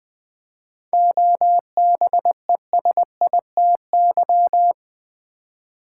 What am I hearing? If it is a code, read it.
Morse code, heard as a keyed tone, about 20 wpm, 700 Hz. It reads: OBESITY
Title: 7 Letter Words - First 3 Letters Given - 20wpm